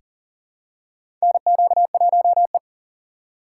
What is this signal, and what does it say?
Morse code, heard as a keyed tone, about 40 wpm, 700 Hz. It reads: NQ1E
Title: USA Call Signs - 4 Characters - 4x Character Spacing - 40wpm